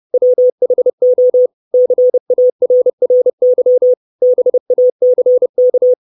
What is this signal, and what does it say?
Morse code, heard as a keyed tone, about 30 wpm, 500 Hz. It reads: WHO CARRY BACK
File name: 632 Words 30WPM